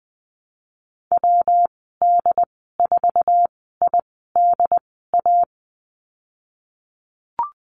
Morse code, heard as a keyed tone, about 20 wpm, spaced as for 15 wpm, 700 Hz. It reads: WD4IDA
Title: 122 Lesson 20g - Number 3 - Call Signs 20wpm